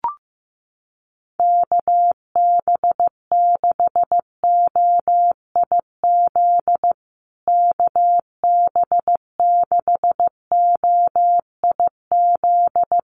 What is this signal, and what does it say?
Morse code, heard as a keyed tone, about 15 wpm, 700 Hz. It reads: KB6OIZ KB6OIZ
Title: USA Call Signs Repeated 2x 15wpm